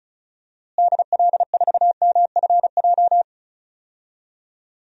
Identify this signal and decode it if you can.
Morse code, heard as a keyed tone, about 36 wpm, 700 Hz. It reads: DL4MFJ
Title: DXCC-100 Call Signs - 6 Characters - Repeated 3x - 35wpm